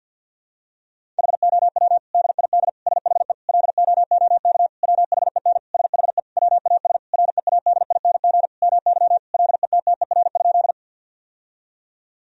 Morse code, heard as a keyed tone, about 50 wpm, 700 Hz. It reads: HOW DID SHE LOOK WHEN SHE WAS READING MY LETTER?
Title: Sentences from Top 1000 Words - Review Part 3 - 50wpm